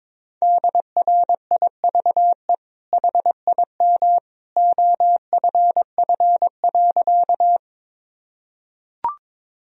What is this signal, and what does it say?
Morse code, heard as a keyed tone, about 22 wpm, 700 Hz. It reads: DRIVE HIM OFF.
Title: Sentences from Top 1000 Words - Review Part 2 - 22wpm